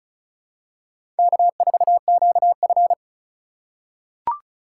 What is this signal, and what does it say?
Morse code, heard as a keyed tone, about 36 wpm, 700 Hz. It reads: K4QF
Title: USA Call Signs - 8x Character Spacing - 35wpm